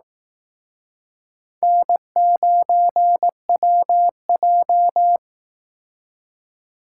Morse code, heard as a keyed tone, about 18 wpm, 700 Hz. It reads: N9WJ
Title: Speed Racing - USA 1x2 Call Signs 15wpm